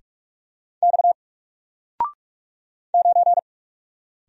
Morse code, heard as a keyed tone, about 44 wpm, 700 Hz. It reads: X 9
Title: Single Letter-Number Alt 45wpm